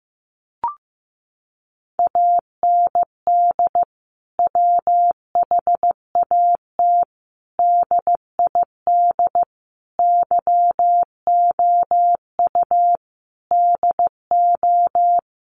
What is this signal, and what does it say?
Morse code, heard as a keyed tone, about 15 wpm, 700 Hz. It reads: AND WHAT DID YOU DO
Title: Sentences from Top 300 Words 15wpm